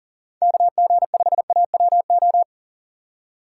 Morse code, heard as a keyed tone, about 40 wpm, 700 Hz. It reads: KG5AWO
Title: USA Call Signs - 6 Characters - 2x Character Spacing - 40wpm